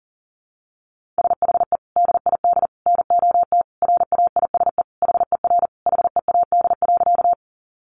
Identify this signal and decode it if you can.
Morse code, heard as a keyed tone, about 40 wpm, 700 Hz. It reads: SHE DID NOT RAISE HER HEAD.
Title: Sentences from Top 1400 Words 40wpm